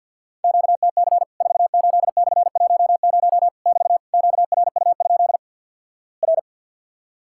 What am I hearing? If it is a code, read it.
Morse code, heard as a keyed tone, about 50 wpm, 700 Hz. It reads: QTC 48/10 <BT> QRU? R
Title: Worked All Europe DX Contest - QTC Exchange - 50wpm